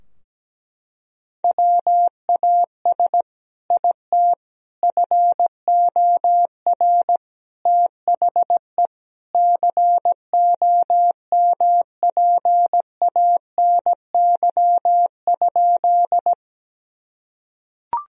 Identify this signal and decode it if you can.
Morse code, heard as a keyed tone, about 17 wpm, 700 Hz. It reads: WAS IT FOR THE COMPANY?
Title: Sentences from Top 1000 Words - Review Part 3 - 17wpm